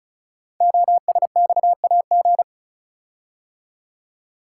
Morse code, heard as a keyed tone, about 36 wpm, 690 Hz. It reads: OSXAG
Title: Code Groups - 5 Letters - 35wpm